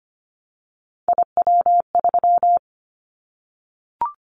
Morse code, heard as a keyed tone, about 25 wpm, 700 Hz. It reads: IW3
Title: Callsign Prefixes for Top 25 DXCC - 25wpm